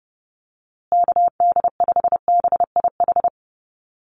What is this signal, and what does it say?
Morse code, heard as a keyed tone, about 30 wpm, 700 Hz. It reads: KD5BIH
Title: USA Call Signs Repeated 3x 30wpm